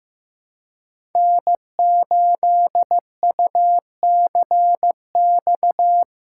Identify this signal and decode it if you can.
Morse code, heard as a keyed tone, about 16 wpm, 700 Hz. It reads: N8UCX
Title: USA Call Signs - 4x Character Spacing - 15wpm